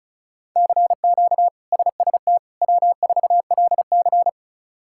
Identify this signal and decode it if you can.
Morse code, heard as a keyed tone, about 36 wpm, 700 Hz. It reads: CQ SST W4LC
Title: K1USN SST QSO Practice - 35wpm